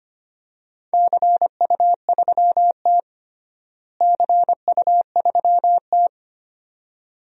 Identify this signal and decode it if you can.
Morse code, heard as a keyed tone, about 25 wpm, 700 Hz. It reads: CU3T CU3T
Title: DXCC-100 Call Signs - 4 Characters - Repeated 2x - 25wpm